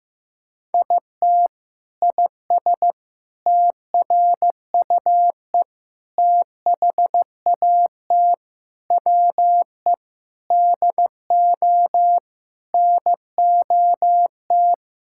Morse code, heard as a keyed tone, about 15 wpm, 700 Hz. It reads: IT IS TRUE THAT WE DO NOT
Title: Sentences from Top 1600 Words 15wpm